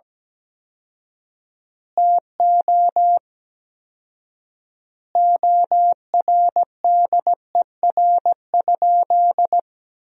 Morse code, heard as a keyed tone, about 17 wpm, 700 Hz. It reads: TO ORDER?
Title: 4x Word-Spacing - Sentences from Top 400 Words - 17wpm